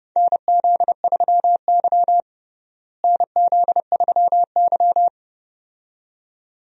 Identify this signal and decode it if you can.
Morse code, heard as a keyed tone, about 30 wpm, 700 Hz. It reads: NZ3Y NZ3Y
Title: USA Call Signs - 4 Characters - Repeated 2x - 30wpm